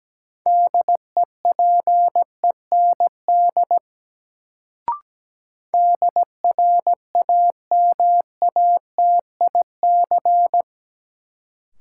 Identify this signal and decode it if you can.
Morse code, heard as a keyed tone, about 17 wpm, 700 Hz. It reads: DEPEND DRAMATIC
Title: Sentences from Top 1300 Words - New Word Drill Alt - 17wpm